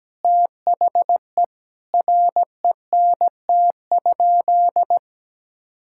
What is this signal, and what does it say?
Morse code, heard as a keyed tone, about 17 wpm, 700 Hz. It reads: THE RENT?
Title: Sentences from Top 1000 Words - Review Part 5 - 17wpm